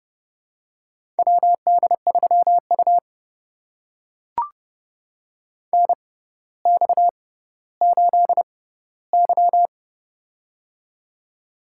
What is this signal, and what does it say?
Morse code, heard as a keyed tone, about 30 wpm, spaced as for 15 wpm, 700 Hz. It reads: WD3U NX8Y
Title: USA Call Signs - 4 Characters - 6x Character Spacing - 30wpm